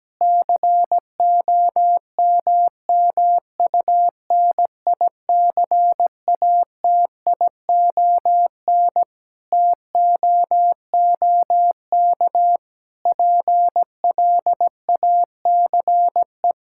Morse code, heard as a keyed tone, about 17 wpm, 700 Hz. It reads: COMMUNICATION TOOK PLACE
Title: Sentences from Top 1000 Words - Review Part 6 - 17wpm